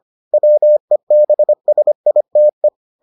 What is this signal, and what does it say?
Morse code, heard as a keyed tone, about 25 wpm, 600 Hz. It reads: WEBSITE